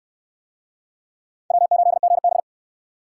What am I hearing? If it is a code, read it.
Morse code, heard as a keyed tone, about 63 wpm, 700 Hz. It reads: W8GB